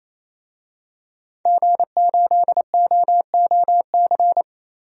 Morse code, heard as a keyed tone, about 28 wpm, 700 Hz. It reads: G8OOC
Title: DXCC-100 Call Signs - 5 Characters - Repeated 3x - 28wpm